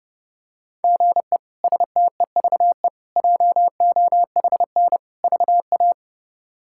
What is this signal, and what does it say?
Morse code, heard as a keyed tone, about 29 wpm, 700 Hz. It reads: GE STEVE JOHN VA
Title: K1USN SST QSO Practice - 30wpm